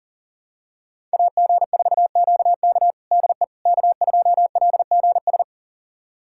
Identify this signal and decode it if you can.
Morse code, heard as a keyed tone, about 40 wpm, 700 Hz. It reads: AG4QK DE K2LGS
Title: USA Callsign DE Callsign - 5 Characters - 40wpm